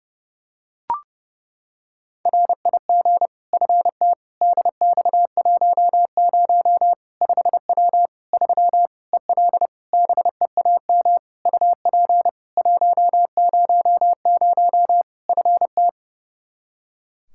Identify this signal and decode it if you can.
Morse code, heard as a keyed tone, about 30 wpm, 700 Hz. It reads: RIG FT DX10 5W 3 EL BEAM UP 100 FT
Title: Practice QSO - Exchange 2 - Info - 30wpm